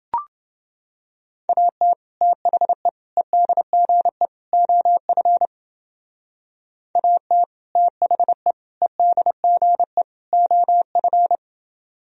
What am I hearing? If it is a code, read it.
Morse code, heard as a keyed tone, about 28 wpm, 700 Hz. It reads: AT THE EDGE OF AT THE EDGE OF